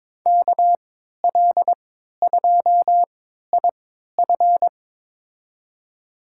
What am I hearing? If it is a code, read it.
Morse code, heard as a keyed tone, about 22 wpm, spaced as for 12 wpm, 700 Hz. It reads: KL2IF
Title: USA Call Signs - 5 Characters - 3x Character Spacing - 22wpm